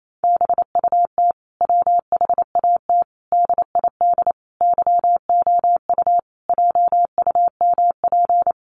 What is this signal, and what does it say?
Morse code, heard as a keyed tone, about 28 wpm, 700 Hz. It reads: BUT WHAT DID YOU JUMP
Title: Sentences from Top 1300 Words 28wpm